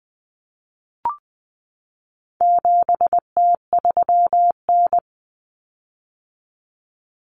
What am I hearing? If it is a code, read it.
Morse code, heard as a keyed tone, about 20 wpm, 700 Hz. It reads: 7T3N
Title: Common Cut Numbers - 4-Digit 20wpm